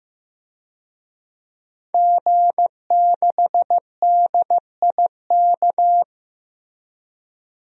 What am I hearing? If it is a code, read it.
Morse code, heard as a keyed tone, about 15 wpm, 700 Hz. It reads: G6DIK
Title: DXCC-100 Call Signs - 5 Characters - Repeated 3x - 15wpm